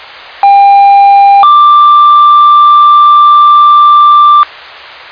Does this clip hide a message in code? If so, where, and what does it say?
Yes: two-tone page from 0.4 to 4.4 s, heard as a one-second tone, then a three-second tone, about 800 then 1180 Hz, with noise.